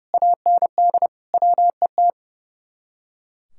Morse code, heard as a keyed tone, about 36 wpm, 700 Hz. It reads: AND WET
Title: Speed-Racing Common QSO Phrases - 30wpm